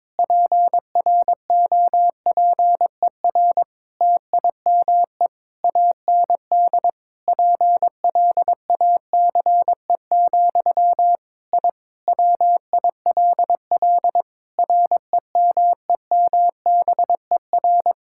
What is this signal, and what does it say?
Morse code, heard as a keyed tone, about 22 wpm, 700 Hz. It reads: PROPER TIME AND PLACE, I WILL REMEMBER
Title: Sentences from Top 1000 Words - Review Part 5 - 22wpm